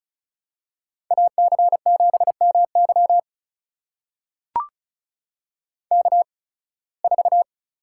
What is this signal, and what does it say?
Morse code, heard as a keyed tone, about 34 wpm, spaced as for 23 wpm, 700 Hz. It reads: AC7MY K4
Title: USA Call Signs - 5 Characters - 8x Character Spacing - 35wpm